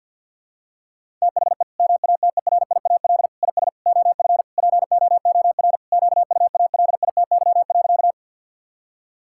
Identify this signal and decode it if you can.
Morse code, heard as a keyed tone, about 52 wpm, 700 Hz. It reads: THE MATERIAL IS OF POOR QUALITY.
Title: Sentences from Top 1000 Words - Review Part 5 - 50wpm